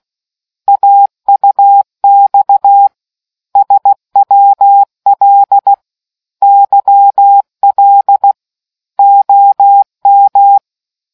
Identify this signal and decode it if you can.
Morse code, heard as a keyed tone, about 16 wpm, 800 Hz. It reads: AUX SWL YL OM